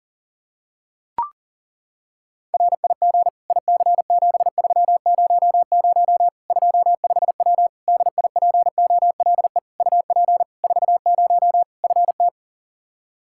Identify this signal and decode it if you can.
Morse code, heard as a keyed tone, about 40 wpm, 700 Hz. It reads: RIG IC7300 25W DIPOLE UP 40 FT
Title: K1USN SST QSO Practice 40wpm